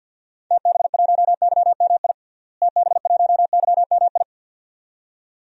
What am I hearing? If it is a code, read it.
Morse code, heard as a keyed tone, about 50 wpm, 700 Hz. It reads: TB1YMI TB1YMI
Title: DXCC-100 Call Signs - 6 Characters - Repeated 2x - 50wpm